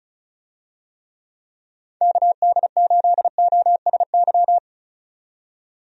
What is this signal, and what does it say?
Morse code, heard as a keyed tone, about 34 wpm, 700 Hz. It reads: KD8OSY
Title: USA Call Signs - 6 Characters - 35wpm